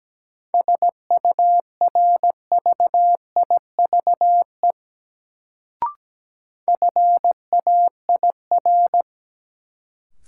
Morse code, heard as a keyed tone, about 17 wpm, 700 Hz. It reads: SURVIVE FAIR
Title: Sentences from Top 1300 Words - New Word Drill Alt - 17wpm